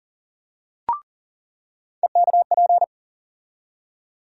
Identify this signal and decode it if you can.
Morse code, heard as a keyed tone, about 40 wpm, 700 Hz. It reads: EKP
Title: USA Call Sign Postfixes - 3 Letters 40wpm